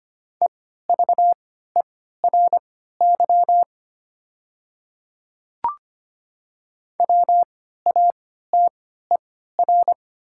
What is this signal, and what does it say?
Morse code, heard as a keyed tone, about 25 wpm, spaced as for 14 wpm, 700 Hz. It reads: EVERY WATER
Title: 3x Character Spacing - Top 200 Words - 25wpm